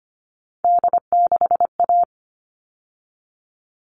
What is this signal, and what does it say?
Morse code, heard as a keyed tone, about 25 wpm, 700 Hz. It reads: D6A
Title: All Cut Numbers - 3-Digit 25wpm